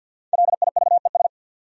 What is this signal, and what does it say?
Morse code, heard as a keyed tone, about 49 wpm, 700 Hz. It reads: LIVES
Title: Sets of 5 Words Encore - New Word Drill - 50wpm